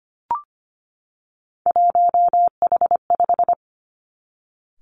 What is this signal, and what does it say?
Morse code, heard as a keyed tone, about 25 wpm, 700 Hz. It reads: 1H5
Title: Three Challenging Letter-Number Combinations 25wpm